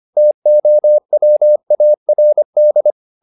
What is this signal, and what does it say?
Morse code, heard as a keyed tone, about 25 wpm, 600 Hz. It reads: TOWARD
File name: the old man and the sea part 1 25wpm